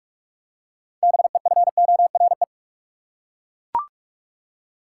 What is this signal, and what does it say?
Morse code, heard as a keyed tone, about 41 wpm, 700 Hz. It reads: BEFORE
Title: Six Letter Words - 45wpm